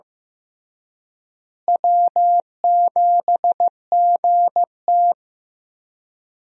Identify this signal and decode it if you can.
Morse code, heard as a keyed tone, about 15 wpm, 700 Hz. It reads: W7GT
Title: USA Call Signs - 4 Characters - Repeated 3x - 15wpm